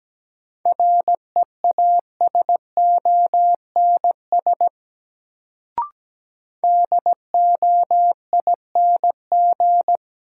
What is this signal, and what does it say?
Morse code, heard as a keyed tone, about 17 wpm, 700 Hz. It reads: REASONS DOING
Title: Sets of 4 Words - New Word Drill - 17wpm